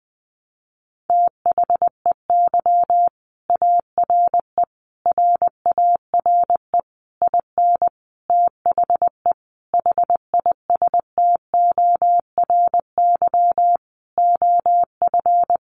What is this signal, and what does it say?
Morse code, heard as a keyed tone, about 19 wpm, 700 Hz. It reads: THEY ARE RARE IN THE HISTORY OF
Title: Sentences from Top 1100 Words 20wpm